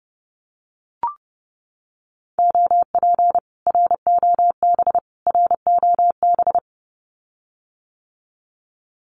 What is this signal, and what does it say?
Morse code, heard as a keyed tone, about 30 wpm, 700 Hz. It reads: OP ROB ROB
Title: Common QSO Phrases Encore 30wpm